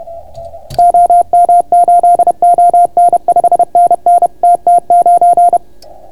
Morse code, heard as a keyed tone, about 30 wpm, 690 Hz. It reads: OM8ON 5NN TT9